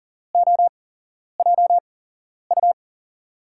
Morse code, heard as a keyed tone, about 39 wpm, spaced as for 11 wpm, 700 Hz. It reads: OJU
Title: USA Call Signs - 6 Characters - 8x Character Spacing - 40wpm